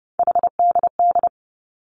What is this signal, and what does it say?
Morse code, heard as a keyed tone, about 30 wpm, 700 Hz. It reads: HDD
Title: Three Challenging Letter-Number Combinations 30wpm